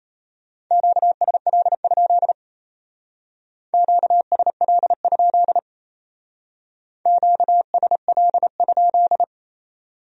Morse code, heard as a keyed tone, about 34 wpm, 700 Hz. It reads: QSL? QSL? QSL?
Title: Speed-Racing Q-Codes - 28wpm